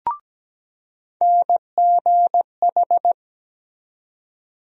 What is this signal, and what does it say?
Morse code, heard as a keyed tone, about 17 wpm, 700 Hz. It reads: NGH
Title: USA Call Sign Postfixes - 3 Letters 17wpm